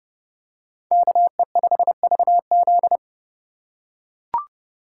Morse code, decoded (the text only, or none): KE5VZ